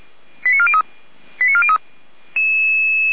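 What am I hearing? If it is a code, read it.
Five-tone selective call (ZVEI), heard as quick runs of five tones, tones of about 70 ms. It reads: 89403 89403
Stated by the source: ZVEI five-tone call received by SDR.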